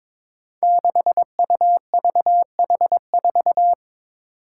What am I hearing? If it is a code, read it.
Morse code, heard as a keyed tone, about 22 wpm, 700 Hz. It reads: 6UVH4
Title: Five Challenging Letter-Number Combinations 22wpm